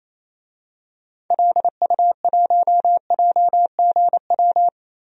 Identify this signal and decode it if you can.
Morse code, heard as a keyed tone, about 28 wpm, 700 Hz. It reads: LU1JGW
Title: DXCC-100 Call Signs - 6 Characters - 28wpm